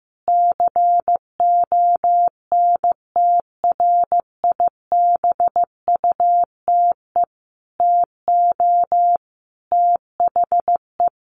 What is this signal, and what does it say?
Morse code, heard as a keyed tone, about 15 wpm, 700 Hz. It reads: CONTRIBUTE TO THE
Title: Sentences from Top 1400 Words 15wpm